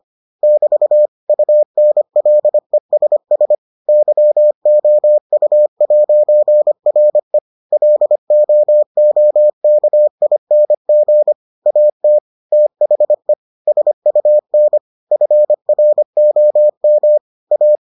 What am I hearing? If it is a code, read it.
Morse code, heard as a keyed tone, about 25 wpm, 600 Hz. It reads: <BT> UNLESS YOU'RE LOOKING AT THE SUN FROM A